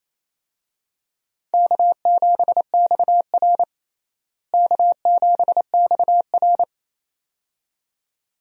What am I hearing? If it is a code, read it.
Morse code, heard as a keyed tone, about 28 wpm, 700 Hz. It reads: K7XR K7XR
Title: USA Call Signs - 4 Characters - Repeated 2x - 28wpm